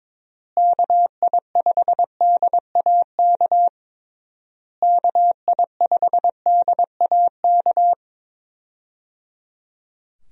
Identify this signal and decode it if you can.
Morse code, heard as a keyed tone, about 22 wpm, 700 Hz. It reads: KI5DAK KI5DAK
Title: USA Call Signs - 6 Characters - Repeated 3x - 22wpm